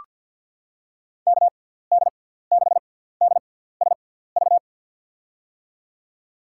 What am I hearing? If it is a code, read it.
Morse code, heard as a keyed tone, about 47 wpm, spaced as for 17 wpm, 700 Hz. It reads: KD6DSV
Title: USA Call Signs - 6 Characters - 6x Character Spacing - 50wpm